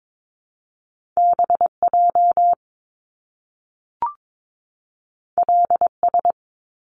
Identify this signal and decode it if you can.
Morse code, heard as a keyed tone, about 22 wpm, 700 Hz. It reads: BJ LS
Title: Top Two Letter Combinations in Words 22wpm